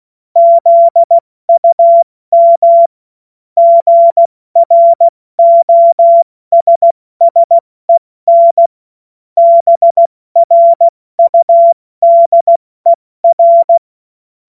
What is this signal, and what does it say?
Morse code, heard as a keyed tone, about 16 wpm, 680 Hz. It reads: ZUM GROSSEN BRUDER
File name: Morsekurs / Witze / Witz 24. August 2021, 70 F80